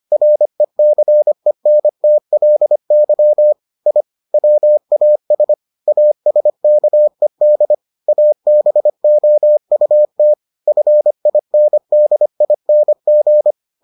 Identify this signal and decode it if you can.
Morse code, heard as a keyed tone, about 25 wpm, 600 Hz. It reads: RECENTLY I WAS ASKED ABOUT FINDING